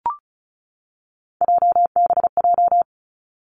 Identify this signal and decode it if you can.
Morse code, heard as a keyed tone, about 36 wpm, 700 Hz. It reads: JBJ
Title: Three Challenging Letter-Number Combinations 35wpm